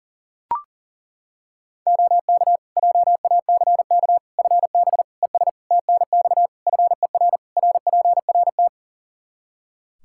Morse code, heard as a keyed tone, about 40 wpm, 700 Hz. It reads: OK JACK FB ES TNX FER RPRT
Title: Practice QSO - Exchange 2 - Info - 40wpm